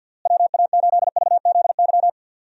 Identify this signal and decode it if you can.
Morse code, heard as a keyed tone, about 51 wpm, 700 Hz. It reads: WA8VZY
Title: USA Call Signs - 6 Characters - Repeated 3x - 50wpm